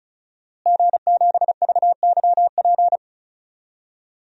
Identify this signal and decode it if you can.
Morse code, heard as a keyed tone, about 36 wpm, 700 Hz. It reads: G7VYP
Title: DXCC-100 Call Signs - 5 Characters - Repeated 2x - 35wpm